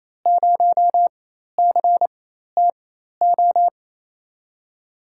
Morse code, heard as a keyed tone, about 28 wpm, spaced as for 13 wpm, 700 Hz. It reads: 0CTO
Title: USA Call Signs - 6 Characters - 4x Character Spacing - 28wpm